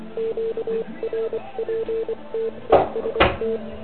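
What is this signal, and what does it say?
Morse code, heard as a keyed tone, about 24 wpm, 440 Hz, with noise. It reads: QRPTEST
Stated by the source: Morse code from the OK QRP zavod 2018 contest, Chrudim.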